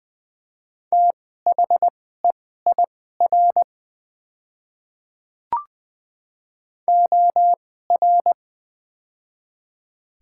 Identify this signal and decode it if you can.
Morse code, heard as a keyed tone, about 21 wpm, spaced as for 15 wpm, 700 Hz. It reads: THEIR OR
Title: 2x Character Spacing - Top 100 Words - 20wpm